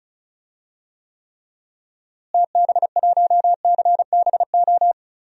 Morse code, heard as a keyed tone, about 34 wpm, 700 Hz. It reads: TB1CBO
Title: DXCC-100 Call Signs - 6 Characters - Repeated 3x - 35wpm